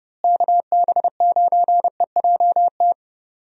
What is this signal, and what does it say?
Morse code, heard as a keyed tone, about 32 wpm, 700 Hz. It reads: KB9EJT
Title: USA Call Signs - 6 Characters - Repeated 3x - 30wpm